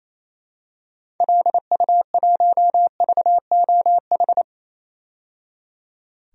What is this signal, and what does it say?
Morse code, heard as a keyed tone, about 28 wpm, 700 Hz. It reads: LU1VOH
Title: DXCC-100 Call Signs - 6 Characters - 28wpm